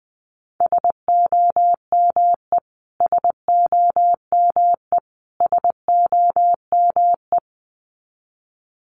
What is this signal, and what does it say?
Morse code, heard as a keyed tone, about 20 wpm, 700 Hz. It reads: SOME SOME SOME
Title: Top 500 Words Repeated 20wpm